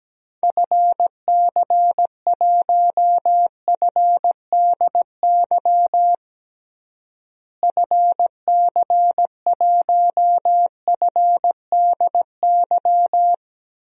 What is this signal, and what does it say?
Morse code, heard as a keyed tone, about 17 wpm, 700 Hz. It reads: FC1FDY FC1FDY
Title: DXCC-100 Call Signs - 6 Characters - Repeated 3x - 17wpm